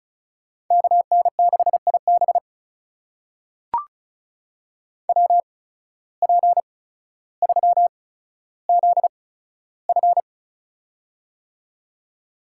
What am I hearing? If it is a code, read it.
Morse code, heard as a keyed tone, about 36 wpm, spaced as for 15 wpm, 700 Hz. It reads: KN6IB WP3ZF
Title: USA Call Signs - 5 Characters - 8x Character Spacing - 35wpm